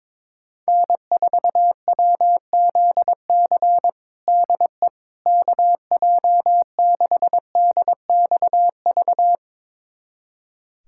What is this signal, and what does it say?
Morse code, heard as a keyed tone, about 22 wpm, 700 Hz. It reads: N4WZC DE KJ6DXV
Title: USA Callsign DE Callsign 22wpm